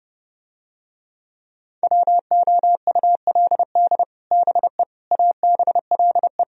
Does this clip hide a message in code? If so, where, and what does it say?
Yes: Morse code from 1.8 to 6.4 s, heard as a keyed tone, about 30 wpm, 700 Hz. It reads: WOULD BE ABLE